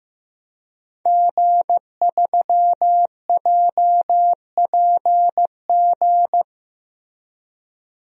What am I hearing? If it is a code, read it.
Morse code, heard as a keyed tone, about 16 wpm, 700 Hz. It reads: G3JPG
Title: DXCC-100 Call Signs - 5 Characters - Repeated 2x - 15wpm